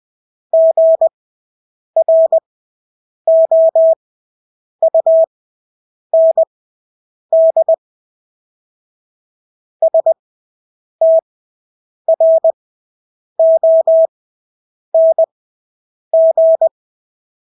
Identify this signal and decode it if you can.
Morse code, heard as a keyed tone, about 20 wpm, spaced as for 8 wpm, 650 Hz. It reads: GROUND STRONG
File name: Engelske ord 20/8 wpm